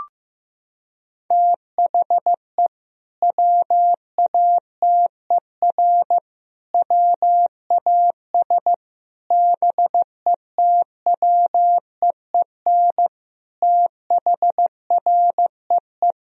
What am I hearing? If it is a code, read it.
Morse code, heard as a keyed tone, about 15 wpm, 700 Hz. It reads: THE WATER WAS BETWEEN THREE
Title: Sentences from Top 1000 Words - Review Part 3 - 15wpm